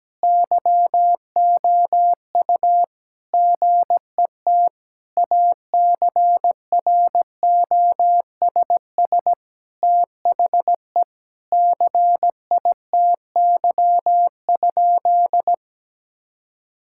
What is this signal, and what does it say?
Morse code, heard as a keyed tone, about 17 wpm, 700 Hz. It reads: YOU GET ACROSS THE CITY?
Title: Sentences from Top 1000 Words - Review Part 5 - 17wpm